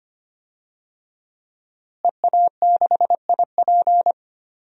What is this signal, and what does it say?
Morse code, heard as a keyed tone, about 25 wpm, 700 Hz. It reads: EA6IP
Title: DXCC-100 Call Signs - 5 Characters - Repeated 3x - 25wpm